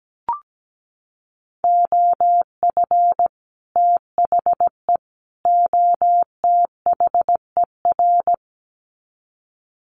Morse code, heard as a keyed tone, about 17 wpm, 700 Hz. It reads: OF THE OTHER
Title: Sets of 3 Words 17wpm